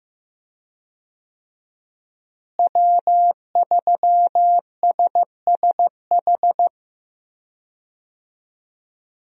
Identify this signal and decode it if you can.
Morse code, heard as a keyed tone, about 15 wpm, 700 Hz. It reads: W3SSH
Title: USA Call Signs - 5 Characters - Repeated 2x - 15wpm